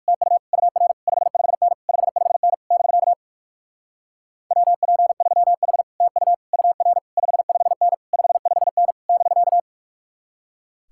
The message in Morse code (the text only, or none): TU UR 55N 55N <BK> WP3H TU UR 55N 55N <BK>